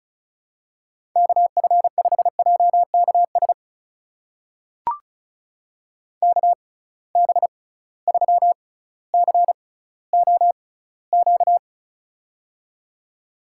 Morse code, heard as a keyed tone, about 34 wpm, spaced as for 18 wpm, 700 Hz. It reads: KF5JKS KB3COQ
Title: USA Call Signs - 6 Characters - 6x Character Spacing - 35wpm